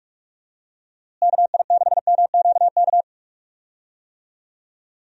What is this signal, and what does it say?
Morse code, heard as a keyed tone, about 41 wpm, 700 Hz. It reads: KI6MQK
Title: USA Call Signs - 6 Characters - Repeated 3x - 45wpm